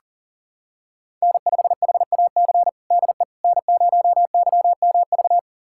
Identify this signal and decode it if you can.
Morse code, heard as a keyed tone, about 40 wpm, 700 Hz. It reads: N5HAC DE N0YMV
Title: USA Callsign DE Callsign - 5 Characters - 40wpm